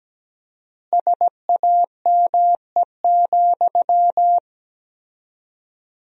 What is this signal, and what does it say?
Morse code, heard as a keyed tone, about 17 wpm, 700 Hz. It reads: SAME,
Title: 4x Word-Spacing - Sentences from Top 300 Words - 17wpm